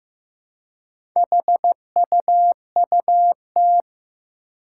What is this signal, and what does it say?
Morse code, heard as a keyed tone, about 15 wpm, 700 Hz. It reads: HUUT
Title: Four Challenging Letter-Number Combinations 15wpm